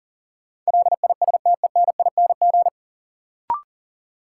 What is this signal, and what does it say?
Morse code, heard as a keyed tone, about 40 wpm, 700 Hz. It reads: LISTENING
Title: Sets of 5 Words - New Word Drill - 40wpm